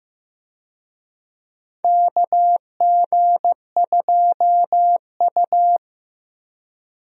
Morse code, heard as a keyed tone, about 15 wpm, 700 Hz. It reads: KG2U
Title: USA Call Signs - 4 Characters - 15wpm